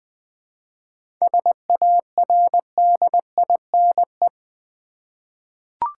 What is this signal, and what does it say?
Morse code, heard as a keyed tone, about 20 wpm, 690 Hz. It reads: SARDINE
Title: Difficult Words with Hints - 20wpm